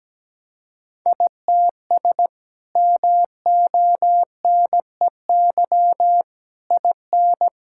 Morse code, heard as a keyed tone, about 17 wpm, 700 Hz. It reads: ITS MONEY IN
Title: Sentences from Top 1000 Words - Review Part 6 - 17wpm